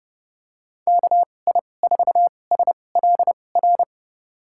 Morse code, heard as a keyed tone, about 29 wpm, spaced as for 22 wpm, 700 Hz. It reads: KI4SLR
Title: USA Call Signs - 6 Characters - 2x Character Spacing - 30wpm